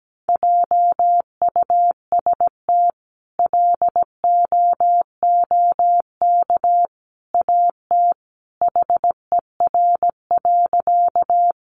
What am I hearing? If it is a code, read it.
Morse code, heard as a keyed tone, about 17 wpm, 700 Hz. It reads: JUST LOOK AT HER.
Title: Sentences from Top 200 Words 17wpm